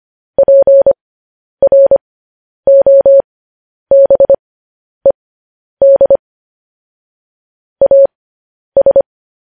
Morse code, heard as a keyed tone, about 25 wpm, spaced as for 10 wpm, 550 Hz. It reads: PROBED AS
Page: news headlines in morse code at 10 wpm